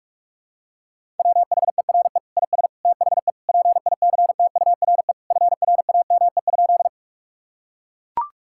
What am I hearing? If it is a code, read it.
Morse code, heard as a keyed tone, about 41 wpm, 700 Hz. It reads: WHERE IS THE PICTURE FRAME?
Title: Sentences from Top 1000 Words - Review Part 6 - 45wpm